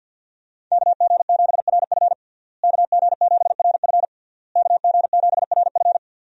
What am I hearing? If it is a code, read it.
Morse code, heard as a keyed tone, about 52 wpm, 700 Hz. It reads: KG7RF KG7RF KG7RF
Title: USA Call Signs - 5 Characters - Repeated 3x - 50wpm